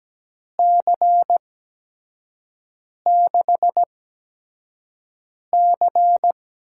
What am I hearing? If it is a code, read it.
Morse code, heard as a keyed tone, about 17 wpm, spaced as for 5 wpm, 700 Hz. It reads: C6C
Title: USA Call Signs - 4 Characters - 8x Character Spacing - 17wpm